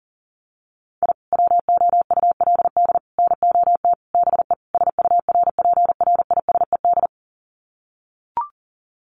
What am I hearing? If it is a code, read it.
Morse code, heard as a keyed tone, about 40 wpm, 700 Hz. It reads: I WOULD NOT BE SURPRISED
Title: Sets of 5 Words Encore2 40wpm